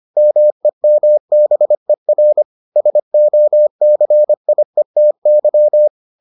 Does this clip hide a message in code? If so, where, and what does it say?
Yes: Morse code from 0.2 to 5.9 s, heard as a keyed tone, about 25 wpm, 600 Hz. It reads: MEMBER SOCIETY